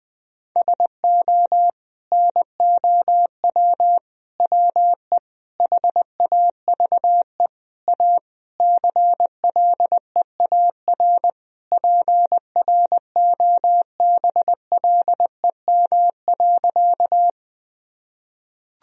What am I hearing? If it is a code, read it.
Morse code, heard as a keyed tone, about 20 wpm, 700 Hz. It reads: SO NOW WE HAVE A CLEAR PROBLEM.
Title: Sentences from Top 1000 Words - Review Part 2 - 20wpm